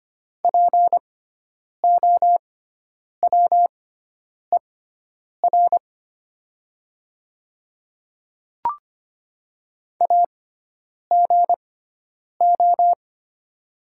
Morse code, heard as a keyed tone, about 25 wpm, spaced as for 9 wpm, 700 Hz. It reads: POWER AGO
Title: 6x Character Spacing - Top 500 Words - 25wpm